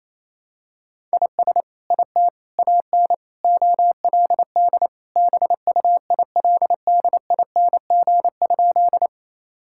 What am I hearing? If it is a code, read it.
Morse code, heard as a keyed tone, about 28 wpm, 700 Hz. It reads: IS IT AN OLD BUILDING?
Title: Sentences from Top 1000 Words - Review Part 5 - 28wpm